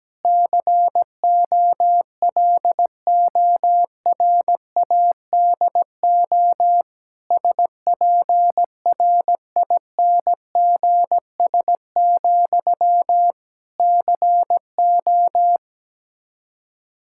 Morse code, heard as a keyed tone, about 17 wpm, 700 Hz. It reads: COLORADO SPRINGS, CO
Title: Practice QSO - Exchange 1 - Beginning - 17wpm